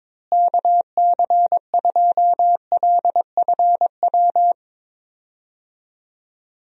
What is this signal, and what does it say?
Morse code, heard as a keyed tone, about 22 wpm, 700 Hz. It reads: KC2LFW
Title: USA Call Signs - 6 Characters - Repeated 2x - 22wpm